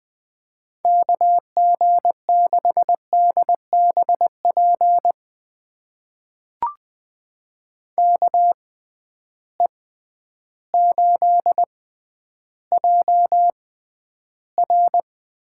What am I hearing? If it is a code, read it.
Morse code, heard as a keyed tone, about 21 wpm, spaced as for 11 wpm, 700 Hz. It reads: KG6DBP KE8JR
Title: USA Call Signs - 6x Character Spacing - 20wpm